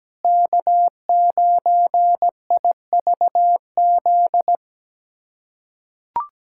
Morse code, heard as a keyed tone, about 17 wpm, 700 Hz. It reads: K9IVZ
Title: USA Callsign DE Callsign - 5 Characters - 17wpm